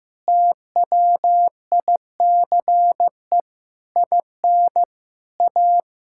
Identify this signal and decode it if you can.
Morse code, heard as a keyed tone, about 15 wpm, 700 Hz. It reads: TWICE IN A
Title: Sentences from Top 1600 Words 15wpm